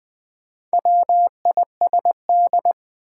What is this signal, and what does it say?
Morse code, heard as a keyed tone, about 20 wpm, 700 Hz. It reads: WISD
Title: Build A Word - 2 Letters At A Time - 6-Letter Words - 20wpm